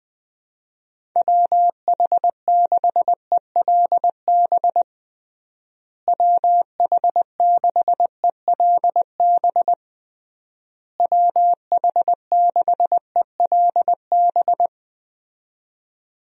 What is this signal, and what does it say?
Morse code, heard as a keyed tone, about 20 wpm, 700 Hz. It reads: WH6ELB WH6ELB WH6ELB
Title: USA Call Signs - 6 Characters - Repeated 3x - 20wpm